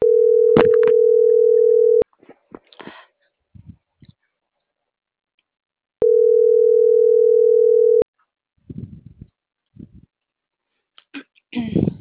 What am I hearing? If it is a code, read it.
Ringback tone (North America), 440 + 480 Hz.